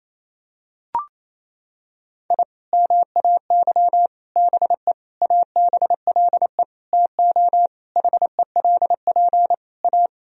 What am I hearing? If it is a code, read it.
Morse code, heard as a keyed tone, about 28 wpm, 700 Hz. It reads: I MAY BE ABLE TO HELP A
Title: Sentences from Top 1000 Words - Review Part 2 - 28wpm